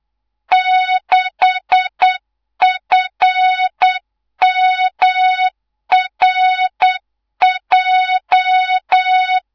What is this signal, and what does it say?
Morse code, heard as a keyed tone, about 8 wpm, 3700 Hz, with noise. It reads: 6FMRJ
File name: Morze-HA2MN / hangp / 2 03 GYAKORLAT